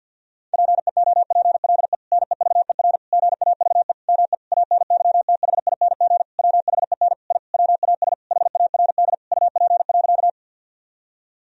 Morse code, heard as a keyed tone, about 50 wpm, 700 Hz. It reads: PEOPLE NEVER GAVE ME ANYTHING WHEN I WAS HARD UP.
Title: Sentences from Top 1000 Words - Review Part 1 - 50wpm